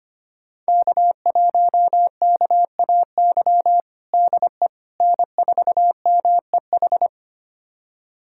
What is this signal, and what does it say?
Morse code, heard as a keyed tone, about 25 wpm, 700 Hz. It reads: K1KAY DE N4MEH
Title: USA Callsign DE Callsign - 5 Characters - 25wpm